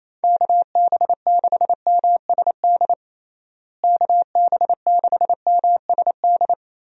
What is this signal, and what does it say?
Morse code, heard as a keyed tone, about 28 wpm, 700 Hz. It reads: KB6MSD KB6MSD
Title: USA Call Signs - 6 Characters - Repeated 3x - 28wpm